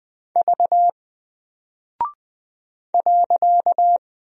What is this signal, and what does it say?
Morse code, heard as a keyed tone, about 20 wpm, 700 Hz. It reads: V .
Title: Single Character (Letter:Number:Punctuation:Prosign) Alt 20wpm